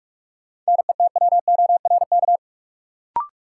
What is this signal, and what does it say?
Morse code, heard as a keyed tone, about 42 wpm, 700 Hz. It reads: NETWORK
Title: Sentences from Top 1000 Words - Review Part 7 - New Word Drill - 45wpm